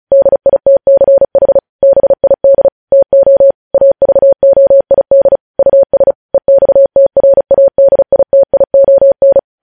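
Morse code, heard as a keyed tone, about 36 wpm, 550 Hz. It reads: DITCH BID TO AVOID US EXTRADITION
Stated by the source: Morse code news headlines sent at 35 words per minute.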